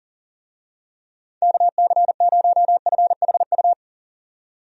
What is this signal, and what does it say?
Morse code, heard as a keyed tone, about 40 wpm, 700 Hz. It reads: KC0FHU
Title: USA Call Signs - 6 Characters - 40wpm